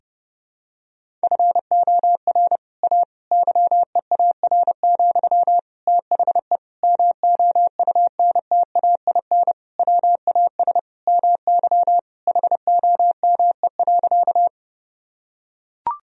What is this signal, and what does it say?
Morse code, heard as a keyed tone, about 30 wpm, 700 Hz. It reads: FOR A YEAR, THE MOUNTAIN WAS MY HOME.
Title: Sentences from Top 1000 Words - Review Part 1 - 30wpm